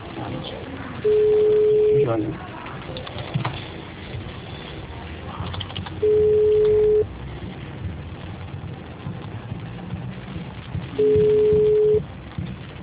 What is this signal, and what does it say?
Ringback tone (Europe, 425 Hz), with speech.